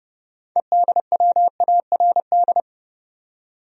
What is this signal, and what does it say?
Morse code, heard as a keyed tone, about 29 wpm, 700 Hz. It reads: EDWARD
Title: USA Common Names - 30wpm